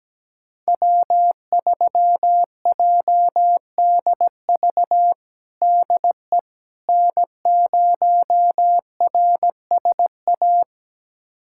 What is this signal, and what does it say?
Morse code, heard as a keyed tone, about 17 wpm, 700 Hz. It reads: W3JDV DE N0RSA
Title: USA Callsign DE Callsign - 5 Characters - 17wpm